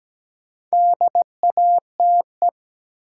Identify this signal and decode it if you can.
Morse code, heard as a keyed tone, about 17 wpm, 700 Hz. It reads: DATE
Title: Sentences from Top 1100 Words - New Word Drill Alt - 17wpm